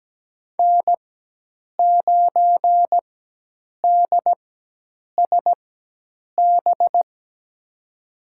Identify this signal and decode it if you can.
Morse code, heard as a keyed tone, about 17 wpm, spaced as for 8 wpm, 700 Hz. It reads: N9DSB
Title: USA Call Signs - 4x Character Spacing - 17wpm